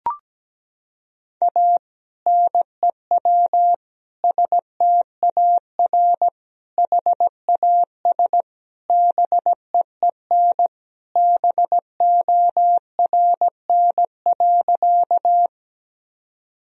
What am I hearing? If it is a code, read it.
Morse code, heard as a keyed tone, about 17 wpm, 700 Hz. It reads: A NEW STAR HAS BEEN BORN.
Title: Sentences from Top 1000 Words - Review Part 2 - 17wpm